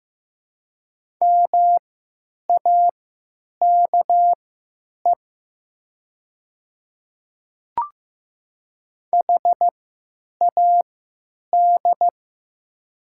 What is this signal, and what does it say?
Morse code, heard as a keyed tone, about 15 wpm, spaced as for 9 wpm, 700 Hz. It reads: MAKE HAD
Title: 3x Character Spacing - Top 100 Words - 15wpm